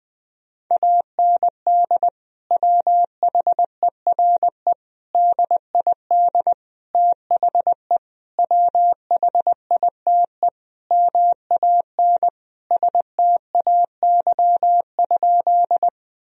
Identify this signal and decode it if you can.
Morse code, heard as a keyed tone, about 20 wpm, 700 Hz. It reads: AND WHERE DID THE WHITE MAN STAY?
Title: Sentences from Top 1000 Words - Review Part 1 - 20wpm